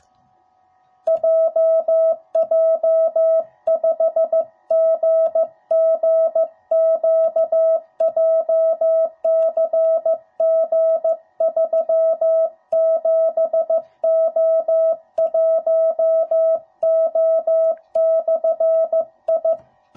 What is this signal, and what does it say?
Morse code, heard as a keyed tone, about 15 wpm, 650 Hz. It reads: JJ5GGQ JCG37O1O/I
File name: jh8cbh / 2021cw / cw05b